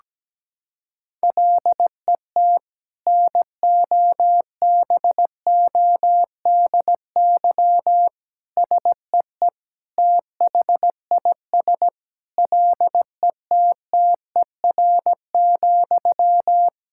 Morse code, heard as a keyed tone, about 17 wpm, 700 Hz. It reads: LET NOBODY SEE THIS LETTER,
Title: Sentences from Top 1600 Words 17wpm